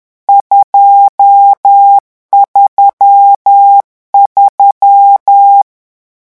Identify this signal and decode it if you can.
Morse code, heard as a keyed tone, about 10 wpm, 800 Hz. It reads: 233
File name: morse mp3 / lecon10-vitesse 10